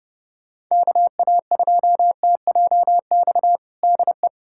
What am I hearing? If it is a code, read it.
Morse code, heard as a keyed tone, about 30 wpm, 700 Hz. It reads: KA2TJX DE
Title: USA Callsign DE Callsign 30wpm